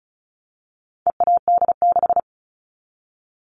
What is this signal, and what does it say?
Morse code, heard as a keyed tone, about 34 wpm, 700 Hz. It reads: EAD6
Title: All Cut Numbers - 4-Digit 35wpm